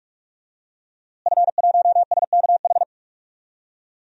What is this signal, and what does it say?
Morse code, heard as a keyed tone, about 46 wpm, 700 Hz. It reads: F1SKH
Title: DXCC-100 Call Signs - 5 Characters - Repeated 3x - 45wpm